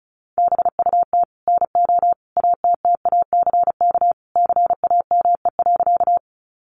Morse code, heard as a keyed tone, about 34 wpm, 700 Hz. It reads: BUT NO ATTACK CAME.
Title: Sentences from Top 1500 Words 35wpm